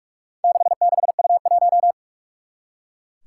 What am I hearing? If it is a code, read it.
Morse code, heard as a keyed tone, about 42 wpm, 700 Hz. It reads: 66U1
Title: Four Challenging Letter-Number Combinations 45wpm